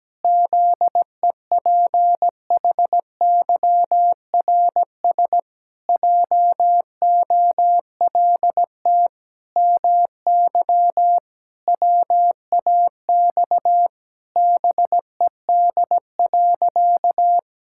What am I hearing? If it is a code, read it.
Morse code, heard as a keyed tone, about 17 wpm, 700 Hz. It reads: ZEPHYRS JOLT MY WAX BED.
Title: Pangram Challenge v2 - 17wpm